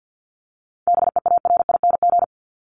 Morse code, heard as a keyed tone, about 50 wpm, 700 Hz. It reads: BEARING
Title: ING Suffix - Top 100 Words with Base Words 50wpm